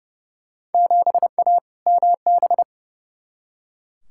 Morse code, heard as a keyed tone, about 30 wpm, 700 Hz. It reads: 7A MB
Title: Field Day Exchanges Encore - 30wpm